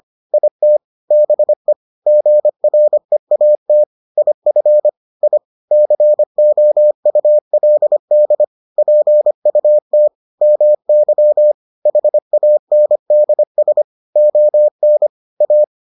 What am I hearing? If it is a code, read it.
Morse code, heard as a keyed tone, about 25 wpm, 600 Hz. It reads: IT BE GREAT IF I COULD PUT MY HANDS ON A